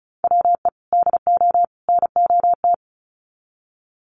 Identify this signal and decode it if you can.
Morse code, heard as a keyed tone, about 34 wpm, 700 Hz. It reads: WE DO NOT
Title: Sets of 3 Words 35wpm